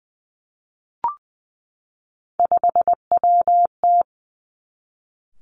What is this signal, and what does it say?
Morse code, heard as a keyed tone, about 20 wpm, 700 Hz. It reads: 5WT
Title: Three Challenging Letter-Number Combinations 20wpm